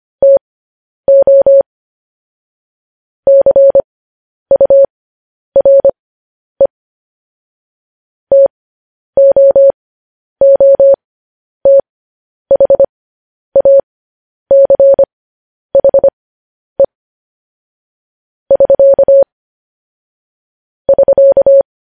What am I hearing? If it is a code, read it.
Morse code, heard as a keyed tone, about 25 wpm, spaced as for 10 wpm, 550 Hz. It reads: TO CURE TOOTHACHE <SK> <SK>
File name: podcast / News headlines-1x-25x10